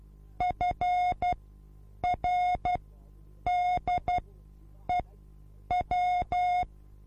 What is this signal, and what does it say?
Morse code, heard as a keyed tone, about 12 wpm, spaced as for 8 wpm, 720 Hz. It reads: FRDEW
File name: repos / Morsekurs / DJ4UF Morsen24